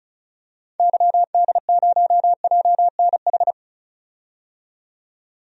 Morse code, heard as a keyed tone, about 36 wpm, 700 Hz. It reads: YD0JNH
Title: DXCC-100 Call Signs - 6 Characters - 35wpm